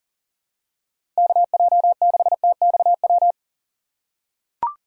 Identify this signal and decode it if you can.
Morse code, heard as a keyed tone, about 40 wpm, 700 Hz. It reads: KJ6TXW
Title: USA Call Signs - 6x Character Spacing - 40wpm